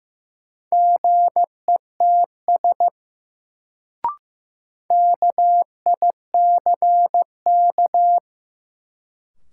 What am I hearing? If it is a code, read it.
Morse code, heard as a keyed tone, about 16 wpm, 700 Hz. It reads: GETS KICK
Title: Sentences from Top 1300 Words - New Word Drill - 15wpm